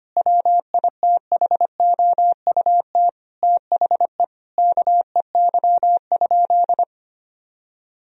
Morse code, heard as a keyed tone, about 25 wpm, 700 Hz. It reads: WITHOUT THE KEY?
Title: Sentences from Top 1000 Words - Review Part 4 - 25wpm